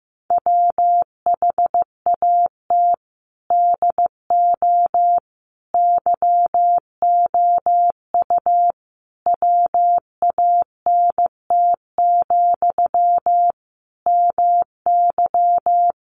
Morse code, heard as a keyed tone, about 15 wpm, 700 Hz. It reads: WHAT DO YOU WANT, MY
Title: Sentences from Top 1000 Words 15wpm